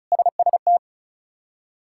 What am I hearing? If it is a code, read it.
Morse code, heard as a keyed tone, about 36 wpm, 700 Hz. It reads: SST
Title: K1USN SST QSO Practice - 35wpm